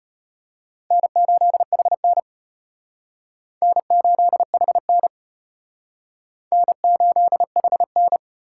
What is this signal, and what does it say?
Morse code, heard as a keyed tone, about 35 wpm, 700 Hz. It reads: N8HN N8HN N8HN